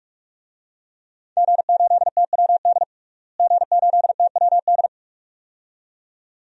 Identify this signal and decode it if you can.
Morse code, heard as a keyed tone, about 46 wpm, 700 Hz. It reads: G8TWD G8TWD
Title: DXCC-100 Call Signs - 5 Characters - Repeated 2x - 45wpm